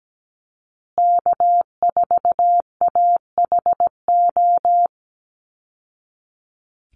Morse code, heard as a keyed tone, about 17 wpm, 700 Hz. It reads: K4AHO
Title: Call Signs Repeated 3x - 17wpm